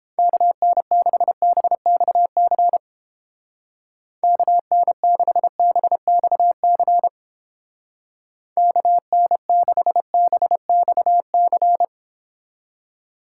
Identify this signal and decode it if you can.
Morse code, heard as a keyed tone, about 29 wpm, 700 Hz. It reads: KN6BXC KN6BXC KN6BXC